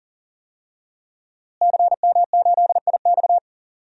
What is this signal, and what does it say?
Morse code, heard as a keyed tone, about 40 wpm, 700 Hz. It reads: CM8IX
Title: DXCC-100 Call Signs - 5 Characters - Repeated 2x - 40wpm